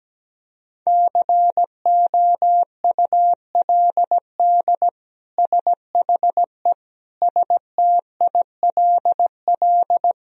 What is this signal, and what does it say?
Morse code, heard as a keyed tone, about 17 wpm, 700 Hz. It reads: COULD SHE STILL
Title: Sentences from Top 1000 Words - Review Part 1 - 17wpm